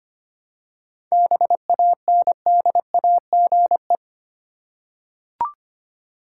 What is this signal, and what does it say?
Morse code, heard as a keyed tone, about 25 wpm, 700 Hz. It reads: BANDAGE
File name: practice / 25wpm / Easy Words with Hints - 25wpm